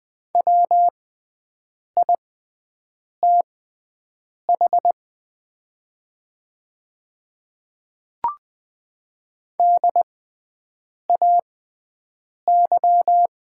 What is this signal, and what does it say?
Morse code, heard as a keyed tone, about 20 wpm, spaced as for 7 wpm, 700 Hz. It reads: WITH DAY
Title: 6x Character Spacing - Top 100 Words - 20wpm